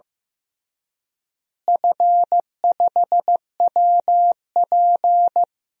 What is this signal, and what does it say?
Morse code, heard as a keyed tone, about 15 wpm, 700 Hz. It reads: F5WP
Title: DXCC-100 Call Signs - 4 Characters - Repeated 3x - 15wpm